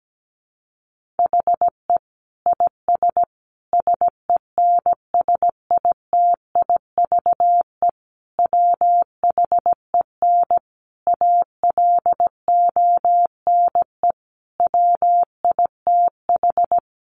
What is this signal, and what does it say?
Morse code, heard as a keyed tone, about 17 wpm, 700 Hz. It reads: HE IS SENSITIVE WHEN ALONE WITH
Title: Sentences from Top 1400 Words 17wpm